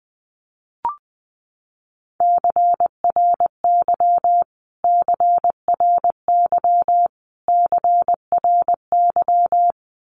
Morse code, heard as a keyed tone, about 20 wpm, 700 Hz. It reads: CRY CRY CRY
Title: Top 500 Words Repeated 20wpm